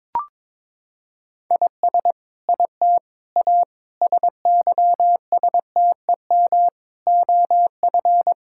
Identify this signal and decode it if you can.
Morse code, heard as a keyed tone, about 22 wpm, 700 Hz. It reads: IS IT A SYSTEM OF
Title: Sentences from Top 1000 Words - Review Part 6 - 22wpm